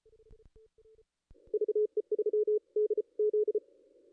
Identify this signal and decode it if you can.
Morse code, heard as a keyed tone, about 36 wpm, spaced as for 31 wpm, 430 Hz. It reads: VE3DZ
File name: contests / ss-cw-k5tr-op-n5rz -2200